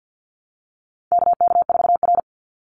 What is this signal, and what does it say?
Morse code, heard as a keyed tone, about 50 wpm, 700 Hz. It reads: KK4R
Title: Call Signs - 50wpm